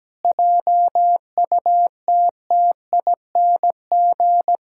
Morse code, heard as a keyed tone, about 17 wpm, 700 Hz. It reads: JUTTING
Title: Pangram Challenge v2 - 17wpm